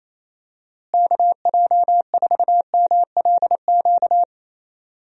Morse code, heard as a keyed tone, about 28 wpm, 700 Hz. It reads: KJ4MLQ
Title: USA Call Signs - 6 Characters - Repeated 3x - 28wpm